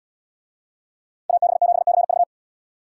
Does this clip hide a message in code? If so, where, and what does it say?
Yes: Morse code from 1.3 to 2.2 s, heard as a keyed tone, about 71 wpm, 700 Hz. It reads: AB7CV